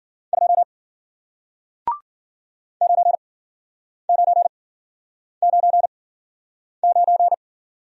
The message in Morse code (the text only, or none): 2 9999